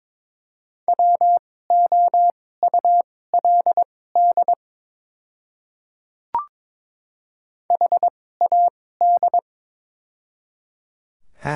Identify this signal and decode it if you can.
Morse code, heard as a keyed tone, about 22 wpm, spaced as for 16 wpm, 690 Hz. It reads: WOULD HAD
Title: 2x Character Spacing - Top 100 Words - 22wpm